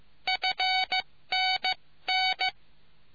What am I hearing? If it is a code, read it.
Morse code, heard as a keyed tone, about 15 wpm, 750 Hz. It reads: FN N